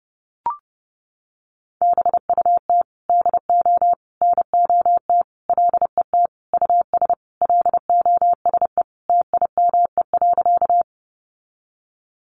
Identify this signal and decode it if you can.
Morse code, heard as a keyed tone, about 30 wpm, 700 Hz. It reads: BUT DO NOT LET US LOSE TIME.
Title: Sentences from Top 1100 Words 30wpm